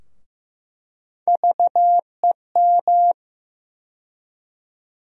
Morse code, heard as a keyed tone, about 15 wpm, 700 Hz. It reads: VEM
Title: USA Call Sign Postfixes - 3 Letters 15wpm